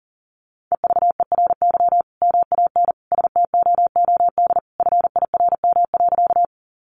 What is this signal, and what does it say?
Morse code, heard as a keyed tone, about 40 wpm, 700 Hz. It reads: EVERY MAN STOOD FIRM.
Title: Sentences from Top 1400 Words 40wpm